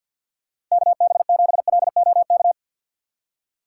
Morse code, heard as a keyed tone, about 54 wpm, 700 Hz. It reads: KB7LOK
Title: USA Call Signs - 6 Characters - 4x Character Spacing - 50wpm